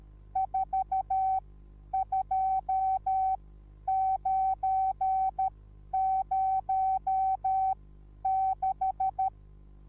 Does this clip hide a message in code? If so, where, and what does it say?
Yes: Morse code from 0.4 to 9.3 s, heard as a keyed tone, about 13 wpm, spaced as for 10 wpm, 760 Hz. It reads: 42906